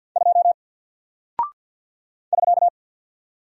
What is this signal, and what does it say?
Morse code, heard as a keyed tone, about 48 wpm, 700 Hz. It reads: 2 <SK>
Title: Single Character (Letter:Number:Punctuation:Prosign) Alt 50wpm